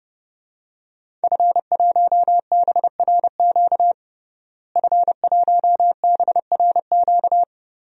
Morse code, heard as a keyed tone, about 32 wpm, 700 Hz. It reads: F1BRQ F1BRQ
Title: DXCC-100 Call Signs - 5 Characters - Repeated 3x - 30wpm